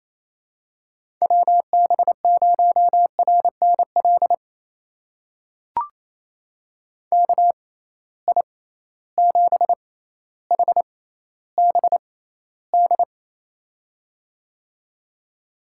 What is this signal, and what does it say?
Morse code, heard as a keyed tone, about 28 wpm, spaced as for 14 wpm, 700 Hz. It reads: WB0RNL KI7HBD
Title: USA Call Signs - 6 Characters - 6x Character Spacing - 28wpm